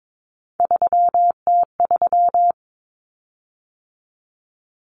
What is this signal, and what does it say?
Morse code, heard as a keyed tone, about 22 wpm, 700 Hz. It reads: 3T3
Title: Common Cut Numbers - 3-Digit 22wpm